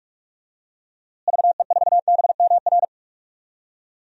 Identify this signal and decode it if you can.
Morse code, heard as a keyed tone, about 46 wpm, 700 Hz. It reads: VE4BMR
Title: DXCC-100 Call Signs - 6 Characters - Repeated 3x - 45wpm